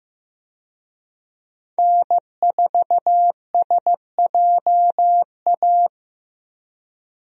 Morse code, heard as a keyed tone, about 15 wpm, 700 Hz. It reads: N4SJA
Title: USA Call Signs - 5 Characters - Repeated 2x - 15wpm